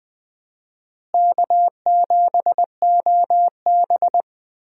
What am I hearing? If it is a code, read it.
Morse code, heard as a keyed tone, about 20 wpm, 700 Hz. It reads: K7OB
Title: USA Call Signs - 4 Characters - Repeated 2x - 20wpm